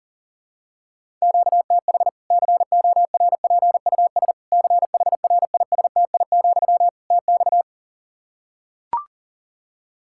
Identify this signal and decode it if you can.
Morse code, heard as a keyed tone, about 40 wpm, 700 Hz. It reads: QTH CORPUS CHRISTI, TX
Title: Practice QSO - Exchange 1 - Beginning - 40wpm